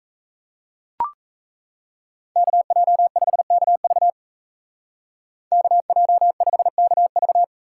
Morse code, heard as a keyed tone, about 37 wpm, 700 Hz. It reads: KJ5KV KJ5KV